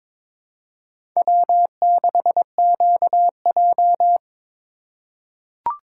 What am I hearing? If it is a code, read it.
Morse code, heard as a keyed tone, about 23 wpm, 700 Hz. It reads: W6QJ
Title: USA Call Signs - 4x Character Spacing - 22wpm